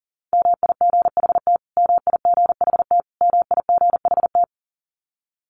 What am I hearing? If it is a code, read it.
Morse code, heard as a keyed tone, about 40 wpm, 700 Hz. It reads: MIGHT MIGHT MIGHT
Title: Top 500 Words Repeated 40wpm